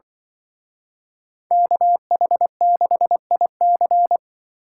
Morse code, heard as a keyed tone, about 26 wpm, 700 Hz. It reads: KH6IC